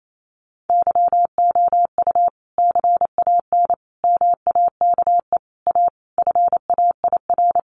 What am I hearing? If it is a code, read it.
Morse code, heard as a keyed tone, about 28 wpm, 700 Hz. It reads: YOU CAN MAKE A FAIR
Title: Sentences from Top 1400 Words 28wpm